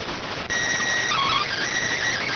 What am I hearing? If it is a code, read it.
SSTV (slow-scan television), PD 290.